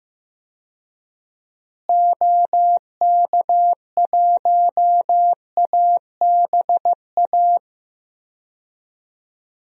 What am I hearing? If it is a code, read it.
Morse code, heard as a keyed tone, about 15 wpm, 700 Hz. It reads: OK1ABA
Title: DXCC-100 Call Signs - 6 Characters - Repeated 3x - 15wpm